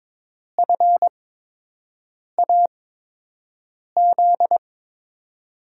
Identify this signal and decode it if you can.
Morse code, heard as a keyed tone, about 22 wpm, spaced as for 6 wpm, 700 Hz. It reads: FAZ
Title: USA Call Signs - 6 Characters - 8x Character Spacing - 22wpm